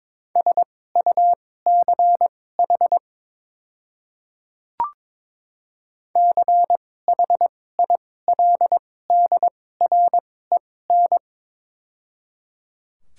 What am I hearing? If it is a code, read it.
Morse code, heard as a keyed tone, about 22 wpm, spaced as for 16 wpm, 700 Hz. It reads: SUCH CHILDREN
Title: 2x Character Spacing - Top 300 Words - 22wpm